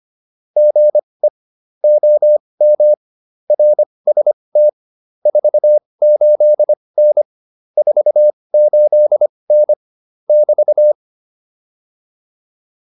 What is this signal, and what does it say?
Morse code, heard as a keyed tone, about 25 wpm, spaced as for 20 wpm, 600 Hz. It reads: GE OM RST 48N 48N <BT>
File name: QSO210a 20